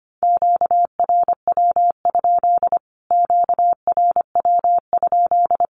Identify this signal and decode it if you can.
Morse code, heard as a keyed tone, about 25 wpm, 700 Hz. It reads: QRW? QRW?
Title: Q-Codes Repeated 3x 25wpm